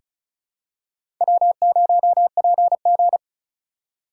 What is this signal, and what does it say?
Morse code, heard as a keyed tone, about 36 wpm, 700 Hz. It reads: W0PG
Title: USA Call Signs - 4 Characters - 3x Character Spacing - 35wpm